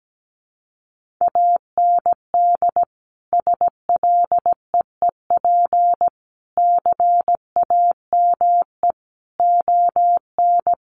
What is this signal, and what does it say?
Morse code, heard as a keyed tone, about 17 wpm, 700 Hz. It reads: AND SLEEP CAME ON
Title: Sentences from Top 1500 Words 17wpm